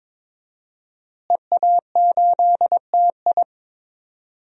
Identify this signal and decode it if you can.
Morse code, heard as a keyed tone, about 23 wpm, 700 Hz. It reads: EA8TI
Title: DXCC-100 Call Signs - 5 Characters - Repeated 2x - 22wpm